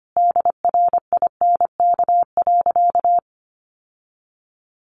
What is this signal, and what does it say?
Morse code, heard as a keyed tone, about 25 wpm, 700 Hz. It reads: DRINK.
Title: Sentences from Top 1500 Words 25wpm